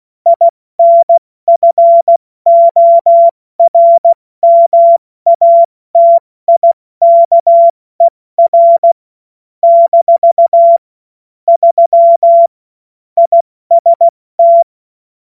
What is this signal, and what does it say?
Morse code, heard as a keyed tone, about 16 wpm, 680 Hz. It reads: INFORMATIKER - 3 IST
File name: Morsekurs / Witze / Witz 12. April 2022 70 F80